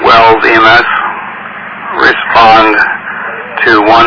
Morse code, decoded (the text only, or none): none